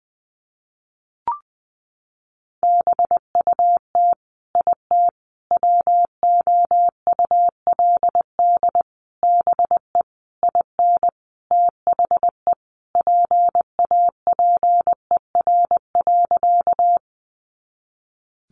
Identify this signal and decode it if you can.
Morse code, heard as a keyed tone, about 20 wpm, 700 Hz. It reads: BUT IT WOULD BE IN THE PAPER.
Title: Sentences from Top 300 Words 20wpm